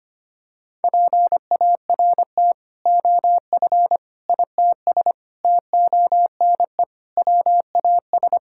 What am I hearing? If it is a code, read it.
Morse code, heard as a keyed tone, about 25 wpm, 700 Hz. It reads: PART OF ITS TONE WAS
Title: Sentences from Top 1600 Words 25wpm